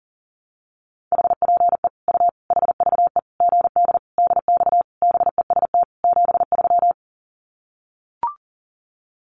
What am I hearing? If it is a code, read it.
Morse code, heard as a keyed tone, about 40 wpm, 700 Hz. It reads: HPE U HVE GD DX BEST 73
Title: Common QSO Phrases 40wpm